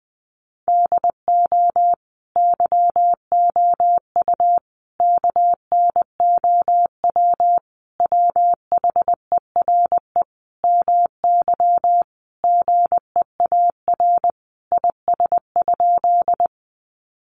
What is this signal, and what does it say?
Morse code, heard as a keyed tone, about 19 wpm, 700 Hz. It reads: DO YOU KNOW WHERE MY GEAR IS?
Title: Sentences from Top 1400 Words 20wpm